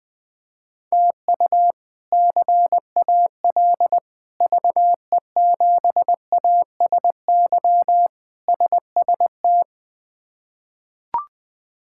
Morse code, heard as a keyed tone, about 20 wpm, 700 Hz. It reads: TU CAL VE7ASY SST
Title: K1USN SST QSO Practice - 20wpm